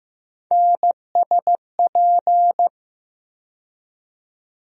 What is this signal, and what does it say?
Morse code, heard as a keyed tone, about 15 wpm, 700 Hz. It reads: NSP
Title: USA Call Sign Postfixes - 3 Letters 15wpm